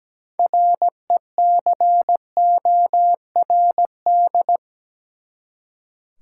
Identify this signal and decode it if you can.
Morse code, heard as a keyed tone, about 17 wpm, 700 Hz. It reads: RECORD
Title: Six Letter Words - 17wpm